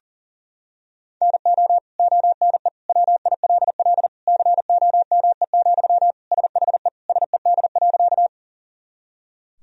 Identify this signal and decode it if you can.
Morse code, heard as a keyed tone, about 39 wpm, 700 Hz. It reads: NO ONE WILL COME, SHE SED.
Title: Sentences from Top 100 Words and Abbreviations 40wpm